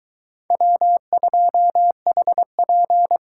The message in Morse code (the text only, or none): W2HP